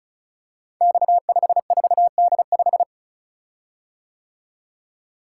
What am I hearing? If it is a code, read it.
Morse code, heard as a keyed tone, about 36 wpm, 700 Hz. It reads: X54D5
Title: Code Groups - 5 Letter:Numbers - 35wpm